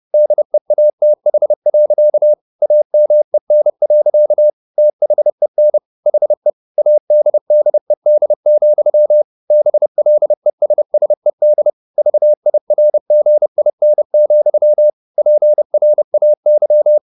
Morse code, heard as a keyed tone, about 30 wpm, 600 Hz. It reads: DEATH. AMEN. THEN HE ADDED, BLESSED VIRGIN, PRAY
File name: the old man and the sea / the old man and the sea 1 30wpm